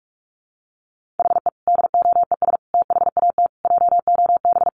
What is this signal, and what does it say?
Morse code, heard as a keyed tone, about 44 wpm, 700 Hz. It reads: HE DOES THAT JOB
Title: Sentences from Top 700 Words 45wpm